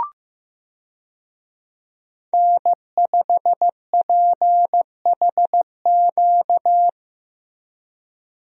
Morse code, heard as a keyed tone, about 15 wpm, 700 Hz. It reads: N5PHQ
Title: USA Call Signs - 5 Characters - Repeated 2x - 15wpm